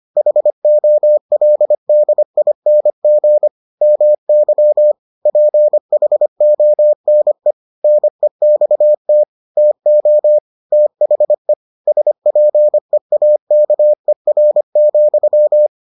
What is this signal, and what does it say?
Morse code, heard as a keyed tone, about 25 wpm, 600 Hz. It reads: HOLDING MY PHONE NEXT TO THE SPEAKER,